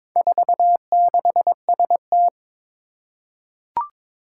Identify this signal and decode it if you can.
Morse code, heard as a keyed tone, about 22 wpm, 700 Hz. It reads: 46ST
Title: Four Challenging Letter-Number Combinations 22wpm